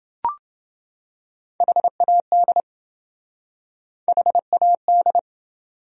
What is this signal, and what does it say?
Morse code, heard as a keyed tone, about 28 wpm, 700 Hz. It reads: HAD HAD